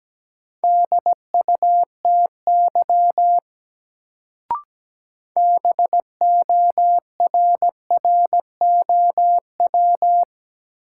Morse code, heard as a keyed tone, about 17 wpm, 700 Hz. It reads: DUTY BORROW
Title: Sentences from Top 1500 Words - New Word Drill Alt - 17wpm